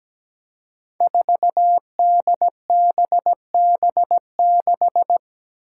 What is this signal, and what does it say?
Morse code, heard as a keyed tone, about 17 wpm, 700 Hz. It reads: 4DBB6
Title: Five Challenging Letter-Number Combinations 17wpm